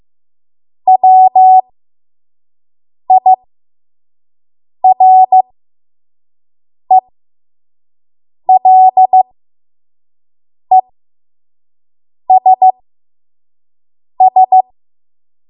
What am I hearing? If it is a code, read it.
Morse code, heard as a keyed tone, about 15 wpm, spaced as for 5 wpm, 750 Hz. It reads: WIRELESS
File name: Morse / 05WPM